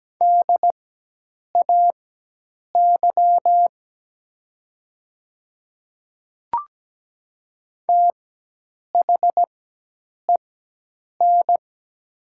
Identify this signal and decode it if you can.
Morse code, heard as a keyed tone, about 17 wpm, spaced as for 8 wpm, 700 Hz. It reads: DAY THEN
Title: 4x Character Spacing - Top 100 Words - 17wpm